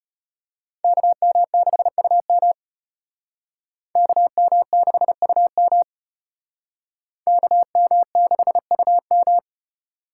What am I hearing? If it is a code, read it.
Morse code, heard as a keyed tone, about 33 wpm, 700 Hz. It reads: KM6UM KM6UM KM6UM